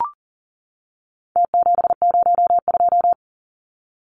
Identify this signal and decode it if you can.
Morse code, heard as a keyed tone, about 40 wpm, 700 Hz. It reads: T702
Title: Numbers - Contest Style - 4-Digit 40wpm